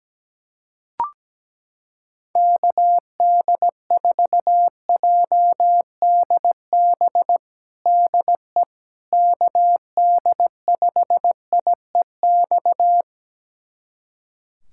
Morse code, heard as a keyed tone, about 17 wpm, 700 Hz. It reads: KD4JDB DE KD5IEX
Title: USA Callsign DE Callsign 17wpm